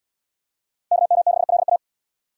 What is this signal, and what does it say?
Morse code, heard as a keyed tone, about 77 wpm, 700 Hz. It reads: KM6LA